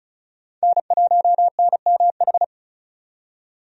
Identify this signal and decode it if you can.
Morse code, heard as a keyed tone, about 36 wpm, 700 Hz. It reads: N1NMH
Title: USA Call Signs - 5 Characters - 35wpm